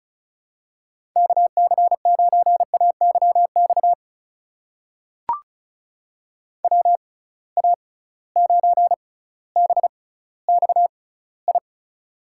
Morse code, heard as a keyed tone, about 36 wpm, spaced as for 18 wpm, 700 Hz. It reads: KC9AYX WA9BXI
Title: USA Call Signs - 6 Characters - 6x Character Spacing - 35wpm